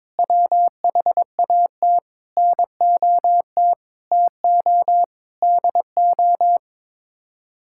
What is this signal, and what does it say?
Morse code, heard as a keyed tone, about 22 wpm, 700 Hz. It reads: WHAT NOT TO DO
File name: practice / 22wpm / Sets of 8 Words - 22wpm